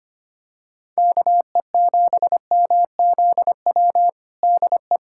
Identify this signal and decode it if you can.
Morse code, heard as a keyed tone, about 25 wpm, 700 Hz. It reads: KE7MZW DE
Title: USA Callsign DE Callsign - 6 Characters - 25wpm